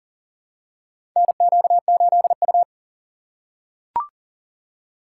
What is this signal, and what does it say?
Morse code, heard as a keyed tone, about 40 wpm, 700 Hz. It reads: NQ8U
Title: USA Call Signs - 4 Characters - 4x Character Spacing - 40wpm